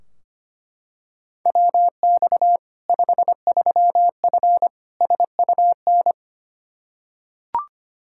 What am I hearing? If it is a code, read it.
Morse code, heard as a keyed tone, about 25 wpm, 700 Hz. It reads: WX 53F SUN
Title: Practice QSO - Exchange 2 - Info - 25wpm